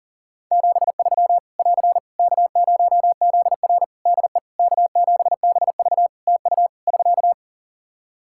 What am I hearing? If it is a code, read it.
Morse code, heard as a keyed tone, about 40 wpm, 700 Hz. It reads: 73 <AR> K0ZR DE K7BV TU <SK>
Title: K1USN SST QSO Practice 40wpm